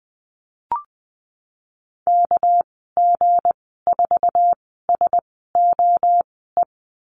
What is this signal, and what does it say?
Morse code, heard as a keyed tone, about 20 wpm, spaced as for 15 wpm, 700 Hz. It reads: KG4SOE
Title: 207 Lesson 33g - Letter j - Call Signs 20wpm